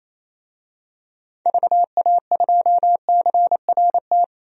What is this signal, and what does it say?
Morse code, heard as a keyed tone, about 28 wpm, 700 Hz. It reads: VA2CRT
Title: DXCC-100 Call Signs - 6 Characters - 28wpm